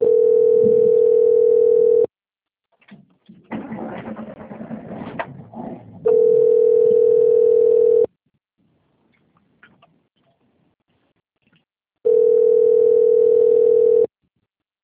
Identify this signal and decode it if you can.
Ringback tone (North America), 440 + 480 Hz.
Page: Capital - SP